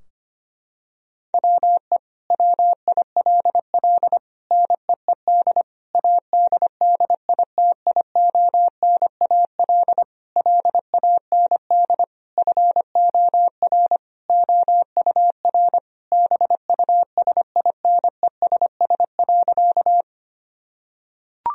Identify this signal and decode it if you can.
Morse code, heard as a keyed tone, about 25 wpm, 700 Hz. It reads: WE WILL NEED ADDITIONAL LAND FOR OUR BUSINESS.
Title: Sentences from Top 1000 Words - Review Part 4 - 25wpm